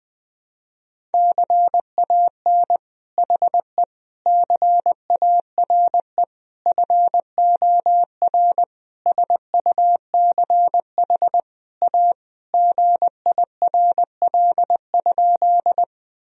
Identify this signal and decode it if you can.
Morse code, heard as a keyed tone, about 19 wpm, 700 Hz. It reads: CAN HE CARE FOR SUCH A GIRL?
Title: Sentences from Top 1000 Words - Review Part 1 - 20wpm